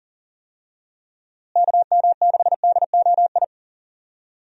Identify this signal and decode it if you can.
Morse code, heard as a keyed tone, about 40 wpm, 700 Hz. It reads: KM6DOI
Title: USA Call Signs - 6 Characters - Repeated 3x - 40wpm